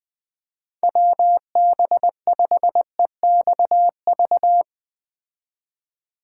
Morse code, heard as a keyed tone, about 20 wpm, 700 Hz. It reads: WB5EXV